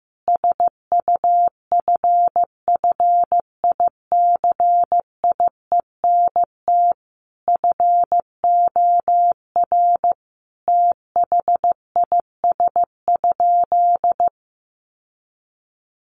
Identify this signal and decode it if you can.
Morse code, heard as a keyed tone, about 14 wpm, 700 Hz. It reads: SUFFICIENT FOR THIS?
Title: Sentences from Top 1400 Words 15wpm